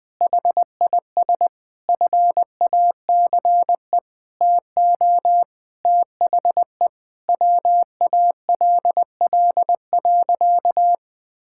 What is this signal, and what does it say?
Morse code, heard as a keyed tone, about 21 wpm, 700 Hz. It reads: HIS FACE TO THE WALL.
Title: Sentences from Top 1000 Words - Review Part 6 - 20wpm